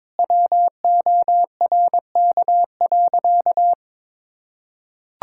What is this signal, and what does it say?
Morse code, heard as a keyed tone, about 22 wpm, 700 Hz. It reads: WORK.
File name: Sentences with 9-Word Phrases - 22wpm